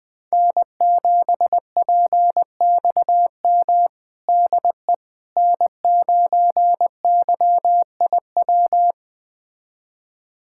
Morse code, heard as a keyed tone, about 20 wpm, 700 Hz. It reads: N7PXM DE N9YIW
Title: USA Callsign DE Callsign - 5 Characters - 20wpm